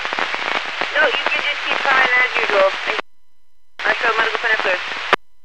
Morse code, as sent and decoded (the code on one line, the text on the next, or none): none